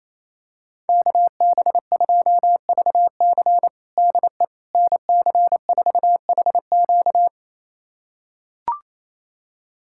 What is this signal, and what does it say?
Morse code, heard as a keyed tone, about 28 wpm, 700 Hz. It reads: KB2VC DE NC4HQ
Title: USA Callsign DE Callsign - 5 Characters - 28wpm